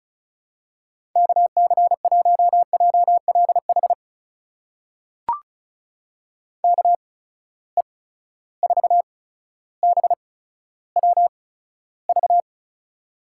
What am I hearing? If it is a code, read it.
Morse code, heard as a keyed tone, about 36 wpm, spaced as for 15 wpm, 700 Hz. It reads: KC1JLH KE4BWV
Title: USA Call Signs - 6 Characters - 8x Character Spacing - 35wpm